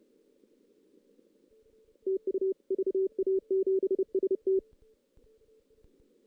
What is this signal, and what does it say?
Morse code, heard as a keyed tone, about 30 wpm, 380 Hz. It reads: TU VA7ST
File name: 2016 JAN CW NAQP / 2016 Jan naqp k5ot - 0-3